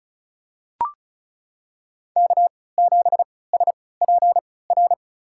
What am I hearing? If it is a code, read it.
Morse code, heard as a keyed tone, about 34 wpm, spaced as for 20 wpm, 700 Hz. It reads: K7SPR
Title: USA Call Signs - 5 Characters - 3x Character Spacing - 35wpm